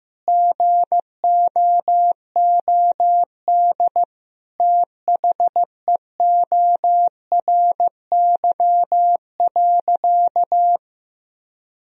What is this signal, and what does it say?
Morse code, heard as a keyed tone, about 15 wpm, 700 Hz. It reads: GOOD THEORY.
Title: Sentences from Top 1000 Words - Review Part 3 - 15wpm